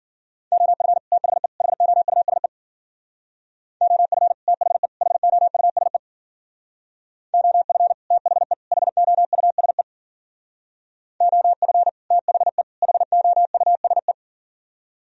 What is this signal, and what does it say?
Morse code, heard as a keyed tone, about 49 wpm, 700 Hz. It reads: OF THE HOUSE OF THE HOUSE OF THE HOUSE OF THE HOUSE
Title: Speed-Racing - Sets of 3 Words 40wpm